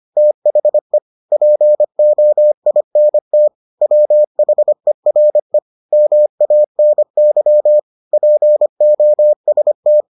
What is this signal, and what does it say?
Morse code, heard as a keyed tone, about 25 wpm, 600 Hz. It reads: THE POINT WHERE MANY POST